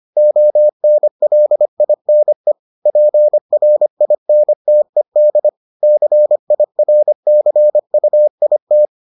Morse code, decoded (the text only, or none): ONLINE PRINTED CIRCUIT